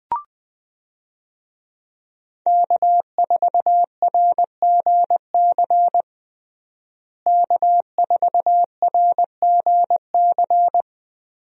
Morse code, heard as a keyed tone, about 21 wpm, 700 Hz. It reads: K4RGC K4RGC
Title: USA Call Signs - 5 Characters - Repeated 2x - 20wpm